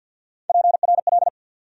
Morse code, heard as a keyed tone, about 51 wpm, 700 Hz. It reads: PRL
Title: USA Call Sign Postfixes - 3 Letters 50wpm